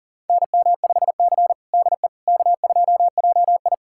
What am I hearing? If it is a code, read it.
Morse code, heard as a keyed tone, about 40 wpm, 700 Hz. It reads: NM5C DE K2JI
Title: USA Callsign DE Callsign - 4 Characters - 40wpm